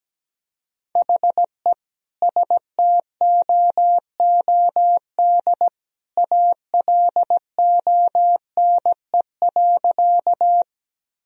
Morse code, heard as a keyed tone, about 17 wpm, 700 Hz. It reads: HE STOOD ALONE.
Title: Sentences from Top 1000 Words - Review Part 2 - 17wpm